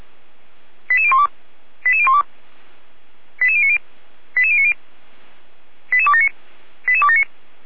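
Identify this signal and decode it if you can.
Five-tone selective call (ZVEI), heard as quick runs of five tones, tones of about 70 ms. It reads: 88012 88012 88090 88090 88279 88279
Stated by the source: ZVEI five-tone selective call received by SDR, fire brigade alerting, Feuerwehr Orlamünde.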